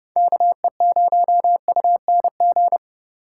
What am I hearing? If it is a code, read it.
Morse code, heard as a keyed tone, about 30 wpm, 700 Hz. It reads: KE0UNG
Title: USA Call Signs - 4x Character Spacing - 30wpm